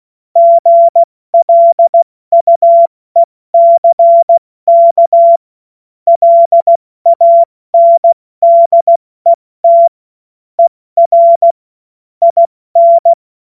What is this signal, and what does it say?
Morse code, heard as a keyed tone, about 16 wpm, 680 Hz. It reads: GLUECK LANDET ER IN
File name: Morsekurs / Witze / Witz 07. Dezember 2021, 70 F80